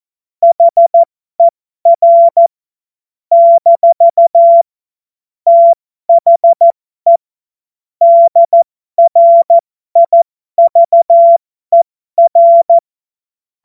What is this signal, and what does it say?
Morse code, heard as a keyed tone, about 14 wpm, spaced as for 12 wpm, 680 Hz. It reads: HER - THE DRIVER
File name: Morsekurs / Witze